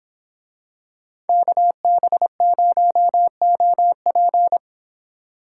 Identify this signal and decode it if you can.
Morse code, heard as a keyed tone, about 25 wpm, 700 Hz. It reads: KB0OP